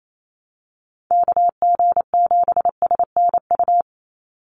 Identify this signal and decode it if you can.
Morse code, heard as a keyed tone, about 28 wpm, 700 Hz. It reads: KG7SNU
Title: USA Call Signs 28wpm